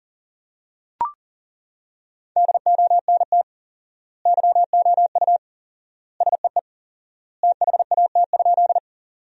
Morse code, heard as a keyed tone, about 40 wpm, 700 Hz. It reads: DONT YOU SEE THAT?
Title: 4x Word-Spacing - Sentences from Top 300 Words - 40wpm